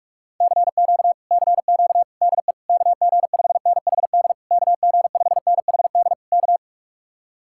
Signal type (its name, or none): Morse code